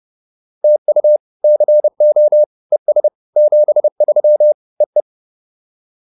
Morse code, heard as a keyed tone, about 30 wpm, 600 Hz. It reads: TU CO ES 73 EE